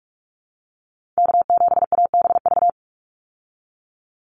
Morse code, heard as a keyed tone, about 44 wpm, 700 Hz. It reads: K7ABV
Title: Call Signs Repeated 3x - 45wpm